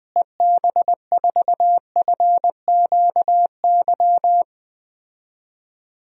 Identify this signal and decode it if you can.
Morse code, heard as a keyed tone, about 20 wpm, 700 Hz. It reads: EB4FQY
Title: DXCC-100 Call Signs - 6 Characters - Repeated 3x - 20wpm